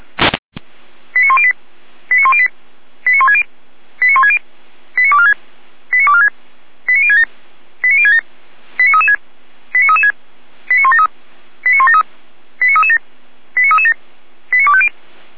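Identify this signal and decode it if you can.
Five-tone selective call (ZVEI), heard as quick runs of five tones, tones of about 70 ms. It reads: 89108 89108 89160 89160 89246 89246 89067 89067 89336 89336 89173 89173 89337 89337 89250
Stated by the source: ZVEI five-tone call received on SDR, channel 455.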